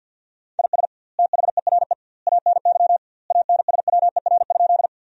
Morse code, heard as a keyed tone, about 52 wpm, 700 Hz. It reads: IS THERE ANY ANSWER?
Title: 2x Word-Spacing - Sentences from Top 300 Words - 50wpm